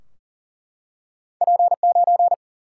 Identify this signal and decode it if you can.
Morse code, heard as a keyed tone, about 41 wpm, 700 Hz. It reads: P9
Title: Single-Letter Single-Number 40wpm